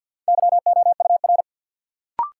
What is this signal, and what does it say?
Morse code, heard as a keyed tone, about 49 wpm, 700 Hz. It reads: YOUR
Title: Sentences from Top 100 Words - New Word Drill - 50wpm